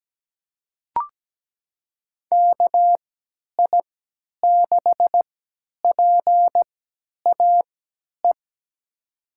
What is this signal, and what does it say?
Morse code, heard as a keyed tone, about 17 wpm, spaced as for 10 wpm, 700 Hz. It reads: KI6PAE
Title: USA Call Signs - 6 Characters - 3x Character Spacing - 17wpm